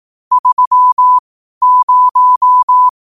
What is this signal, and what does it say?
Morse code, heard as a keyed tone, about 18 wpm, spaced as for 12 wpm, 1000 Hz. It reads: 30
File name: qotdfiles / 12wpm